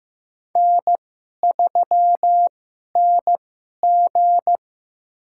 Morse code, heard as a keyed tone, about 15 wpm, spaced as for 11 wpm, 700 Hz. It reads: N3NG
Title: USA Call Signs - 2x Character Spacing - 15wpm